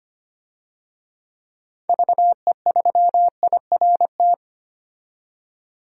Morse code, heard as a keyed tone, about 25 wpm, 700 Hz. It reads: VE3IRT
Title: DXCC-100 Call Signs - 6 Characters - 25wpm